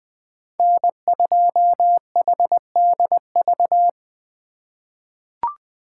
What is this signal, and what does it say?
Morse code, heard as a keyed tone, about 20 wpm, 700 Hz. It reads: N2HDV
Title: USA Call Signs - 5 Characters - 4x Character Spacing - 20wpm